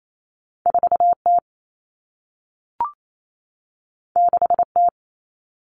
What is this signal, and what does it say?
Morse code, heard as a keyed tone, about 28 wpm, 700 Hz. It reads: 4T 6T
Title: All Cut Numbers - 2-Digit 28wpm